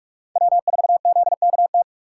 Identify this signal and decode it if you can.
Morse code, heard as a keyed tone, about 42 wpm, 700 Hz. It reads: W4ZKT
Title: USA Call Signs - 5 Characters - 6x Character Spacing - 45wpm